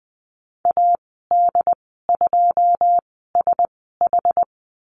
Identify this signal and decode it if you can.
Morse code, heard as a keyed tone, about 20 wpm, spaced as for 14 wpm, 700 Hz. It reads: AD2SH
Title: 082 Lesson 14g - Number 2 - Call Signs 20wpm